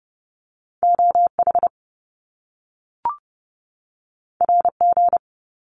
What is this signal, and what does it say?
Morse code, heard as a keyed tone, about 30 wpm, 700 Hz. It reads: OH RG
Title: Top Two Letter Combinations in Words 30wpm